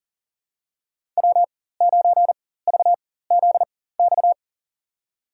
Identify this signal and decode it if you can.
Morse code, heard as a keyed tone, about 40 wpm, spaced as for 19 wpm, 700 Hz. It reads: W9VZX
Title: USA Call Signs - 5 Characters - 4x Character Spacing - 40wpm